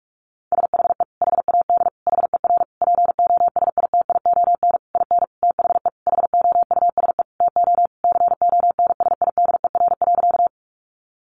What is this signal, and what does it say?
Morse code, heard as a keyed tone, about 44 wpm, 700 Hz. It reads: SHE HAD HER POSITION IN THE HOUSE TO CONSIDER.
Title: Sentences from Top 900 Words 45wpm